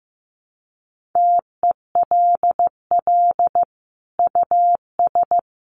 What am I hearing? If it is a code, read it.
Morse code, heard as a keyed tone, about 15 wpm, 700 Hz. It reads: TELL US
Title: Sentences from Top 1400 Words 15wpm